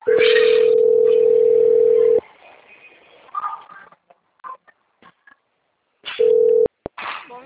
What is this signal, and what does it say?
Ringback tone (North America), 440 + 480 Hz.